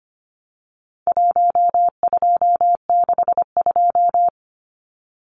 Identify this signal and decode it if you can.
Morse code, heard as a keyed tone, about 25 wpm, 700 Hz. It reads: 1262
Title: Numbers - 4-Digit 25wpm